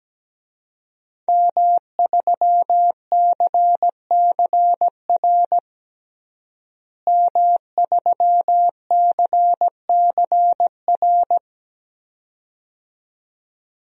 Morse code, heard as a keyed tone, about 17 wpm, 700 Hz. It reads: M3CCR M3CCR
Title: DXCC-100 Call Signs - 5 Characters - Repeated 2x - 17wpm